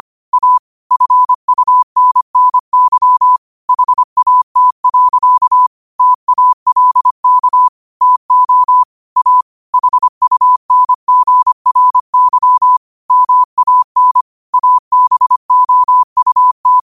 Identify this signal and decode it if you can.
Morse code, heard as a keyed tone, about 25 wpm, 1000 Hz. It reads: A FUNNY HAT. TALK TO A HUNGRY MAN ABOUT